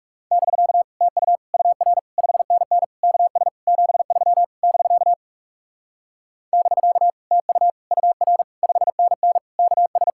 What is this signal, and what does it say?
Morse code, heard as a keyed tone, about 40 wpm, 700 Hz. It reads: <BK> TU UR 5NN KS 73 <BK> <BK> TU UR 5NN KS